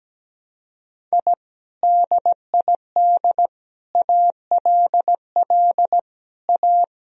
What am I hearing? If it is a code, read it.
Morse code, heard as a keyed tone, about 17 wpm, 700 Hz. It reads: I DID ALL A
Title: Sentences from Top 1000 Words - Review Part 1 - 17wpm